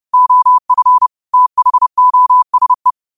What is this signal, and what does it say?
Morse code, heard as a keyed tone, about 30 wpm, 1000 Hz. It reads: OF THOSE